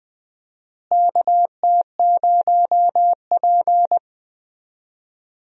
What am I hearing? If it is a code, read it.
Morse code, heard as a keyed tone, about 19 wpm, 700 Hz. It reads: KT0P
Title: USA Call Signs - 4 Characters - Repeated 3x - 20wpm